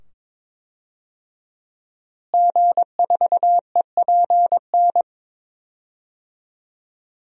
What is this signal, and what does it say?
Morse code, heard as a keyed tone, about 22 wpm, 700 Hz. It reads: G4EPN
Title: DXCC-100 Call Signs - 5 Characters - Repeated 2x - 22wpm